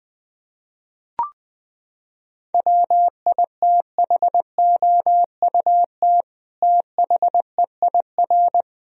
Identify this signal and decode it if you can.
Morse code, heard as a keyed tone, about 21 wpm, 700 Hz. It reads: WITHOUT THEIR
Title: Sentences from Top 1000 Words - Review Part 3 - 20wpm